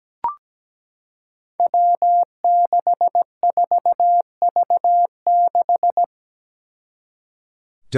Morse code, heard as a keyed tone, about 17 wpm, 700 Hz. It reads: W64V6
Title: Five Challenging Letter-Number Combinations 17wpm